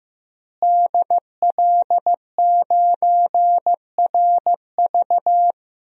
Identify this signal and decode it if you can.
Morse code, heard as a keyed tone, about 16 wpm, 700 Hz. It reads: DL9RV
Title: DXCC-100 Call Signs - 5 Characters - Repeated 2x - 15wpm